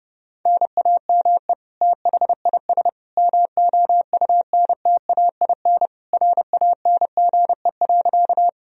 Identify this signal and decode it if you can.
Morse code, heard as a keyed tone, about 28 wpm, 700 Hz. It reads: NAME THIS MOUNTAIN RANGE.
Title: Sentences from Top 1000 Words - Review Part 4 - 30wpm